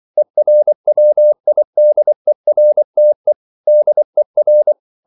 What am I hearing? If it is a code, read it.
Morse code, heard as a keyed tone, about 24 wpm, 600 Hz. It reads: ERWIDERTE DER